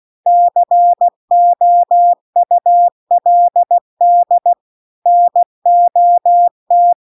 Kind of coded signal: Morse code